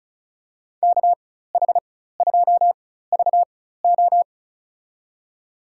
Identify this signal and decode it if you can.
Morse code, heard as a keyed tone, about 34 wpm, spaced as for 16 wpm, 700 Hz. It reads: KH2VO
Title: USA Call Signs - 5 Characters - 4x Character Spacing - 35wpm